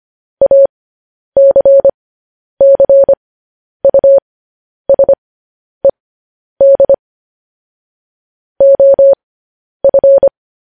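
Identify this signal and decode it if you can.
Morse code, heard as a keyed tone, about 25 wpm, spaced as for 10 wpm, 550 Hz. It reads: ACCUSED OF
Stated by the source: Morse code news headlines sent at 10 wpm for practice.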